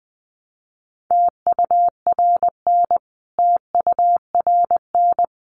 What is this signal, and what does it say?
Morse code, heard as a keyed tone, about 20 wpm, 700 Hz. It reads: TURN TURN
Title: Top 200 Words Repeated 20wpm